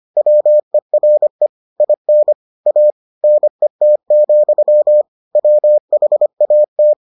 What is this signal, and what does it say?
Morse code, heard as a keyed tone, about 25 wpm, 600 Hz. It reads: WERE IN A NET, WHAT